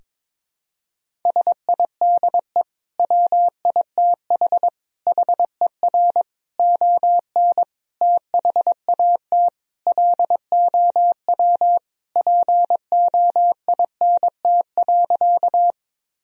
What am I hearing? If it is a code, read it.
Morse code, heard as a keyed tone, about 22 wpm, 700 Hz. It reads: SIDE WITH HER ON THAT LOW POINT.
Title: Sentences from Top 1000 Words - Review Part 1 - 22wpm